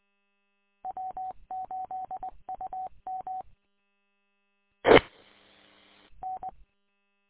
Morse code, heard as a keyed tone, about 20 wpm, 750 Hz. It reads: W8UM T N
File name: w8um- -combined